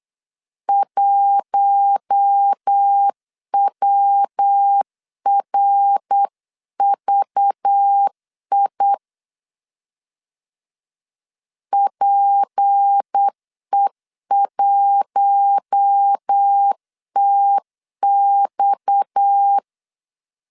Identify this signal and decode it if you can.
Morse code, heard as a keyed tone, about 8 wpm, 800 Hz. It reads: 1WRVI PE1TX
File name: LEZIONI / 6 PKX0 / 6 PKX 8wpm 3041